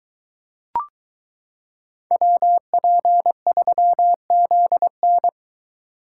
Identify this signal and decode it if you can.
Morse code, heard as a keyed tone, about 22 wpm, 700 Hz. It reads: WP3ZN